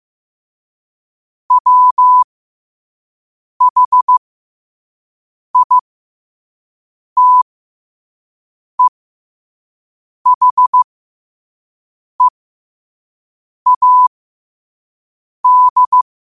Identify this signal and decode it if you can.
Morse code, heard as a keyed tone, about 15 wpm, spaced as for 5 wpm, 1000 Hz. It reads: WHITEHEAD
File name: qotdfiles / 5wpm